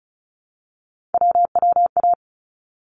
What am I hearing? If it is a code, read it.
Morse code, heard as a keyed tone, about 34 wpm, 700 Hz. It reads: WWA
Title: ARRL Sections 35wpm